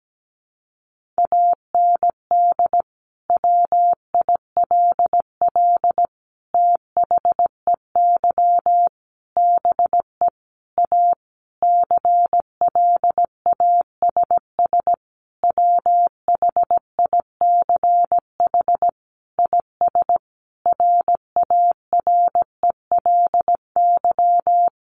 Morse code, heard as a keyed tone, about 17 wpm, 700 Hz. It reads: AND WILL THEY BE A CLASS WHICH IS RARELY
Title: Sentences from Top 1400 Words 17wpm